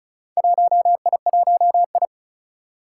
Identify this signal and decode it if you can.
Morse code, heard as a keyed tone, about 36 wpm, 700 Hz. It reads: 1I1I
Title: Four Challenging Letter-Number Combinations 35wpm